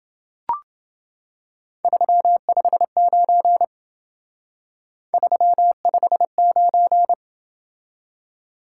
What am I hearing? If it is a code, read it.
Morse code, heard as a keyed tone, about 28 wpm, 700 Hz. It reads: 359 359